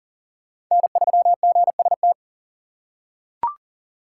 Morse code, heard as a keyed tone, about 40 wpm, 700 Hz. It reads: N3GST
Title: USA Call Signs - 5 Characters - 6x Character Spacing - 40wpm